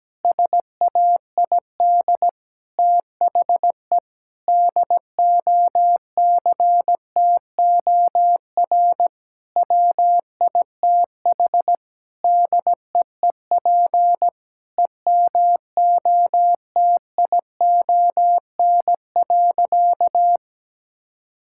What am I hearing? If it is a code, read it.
Morse code, heard as a keyed tone, about 17 wpm, 700 Hz. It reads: SAID THE DOCTOR WITH DEEP EMOTION.
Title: Sentences from Top 1600 Words 17wpm